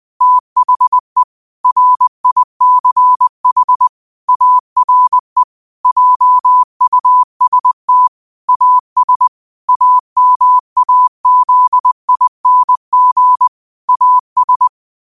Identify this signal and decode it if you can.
Morse code, heard as a keyed tone, about 20 wpm, 1000 Hz. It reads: THE RICH ARE JUST AS AMAZING AS
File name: qotdfiles / 20wpm